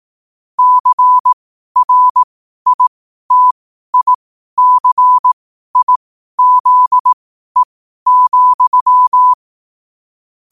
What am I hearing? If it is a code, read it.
Morse code, heard as a keyed tone, about 18 wpm, spaced as for 12 wpm, 1000 Hz. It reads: CRITICIZE,